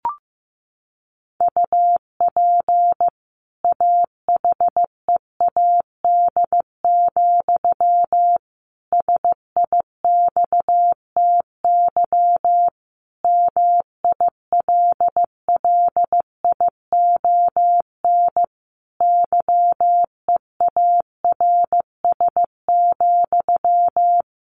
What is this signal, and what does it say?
Morse code, heard as a keyed tone, about 15 wpm, 700 Hz. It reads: UP AHEAD, SIXTY MILLION YEARS,
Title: A Sound Of Thunder 15wpm